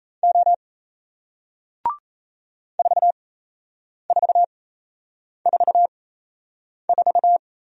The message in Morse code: O 4444

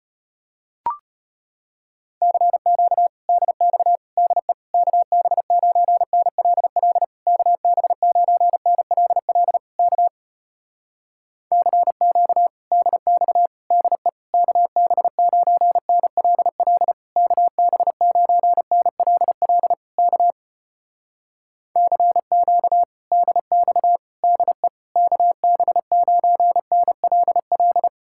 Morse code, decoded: CQ DX DE KB9NLL KB9NLL K CQ DX DE KB9NLL KB9NLL K CQ DX DE KB9NLL

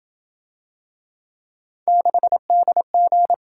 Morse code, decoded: 6DG

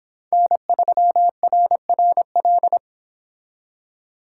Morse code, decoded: N3RRL

